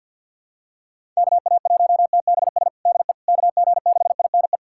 Morse code, heard as a keyed tone, about 47 wpm, 700 Hz. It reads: KA1TBS DE KG6INE
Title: USA Callsign DE Callsign 50wpm